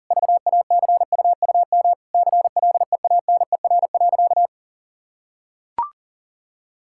Morse code, heard as a keyed tone, about 40 wpm, 700 Hz. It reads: VACUUM CLEANER.
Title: Pangram Challenge v2 - 40wpm